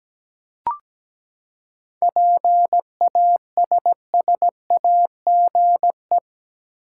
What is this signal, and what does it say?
Morse code, heard as a keyed tone, about 17 wpm, 700 Hz. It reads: PASSAGE
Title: Seven Letter Words - 17wpm